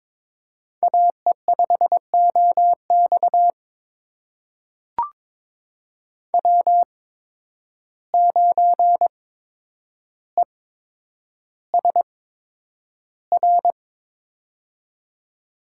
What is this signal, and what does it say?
Morse code, heard as a keyed tone, about 22 wpm, spaced as for 9 wpm, 700 Hz. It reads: AE5OX W9ESR
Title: USA Call Signs - 5 Characters - 8x Character Spacing - 22wpm